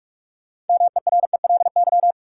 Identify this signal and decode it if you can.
Morse code, heard as a keyed tone, about 42 wpm, 700 Hz. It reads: MERELY
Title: Sentences from Top 1400 Words - New Word Drill - 45wpm